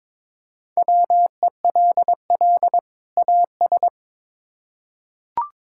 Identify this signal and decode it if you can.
Morse code, heard as a keyed tone, about 22 wpm, 700 Hz. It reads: WELL AS
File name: Build A Phrase - 2-Word Phrases - 22wpm